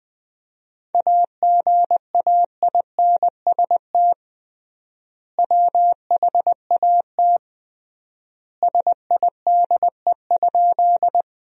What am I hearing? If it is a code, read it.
Morse code, heard as a keyed tone, about 20 wpm, 700 Hz. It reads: AGAINST WHAT SIDE?
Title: 3x Word-Spacing - Sentences from Top 400 Words - 20wpm